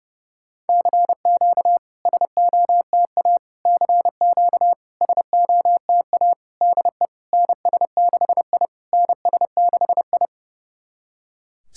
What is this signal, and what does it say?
Morse code, heard as a keyed tone, about 28 wpm, 690 Hz. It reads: CQ SOTA CQ SOTA DE NS6I NS6I
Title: SOTA QSO Practice 30wpm